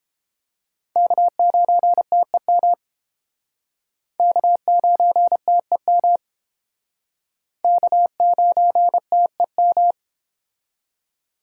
Morse code, heard as a keyed tone, about 31 wpm, 700 Hz. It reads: K9TEM K9TEM K9TEM